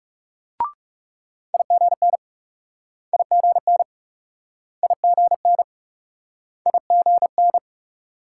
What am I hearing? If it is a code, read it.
Morse code, heard as a keyed tone, about 38 wpm, 700 Hz. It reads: IGN IGN IGN IGN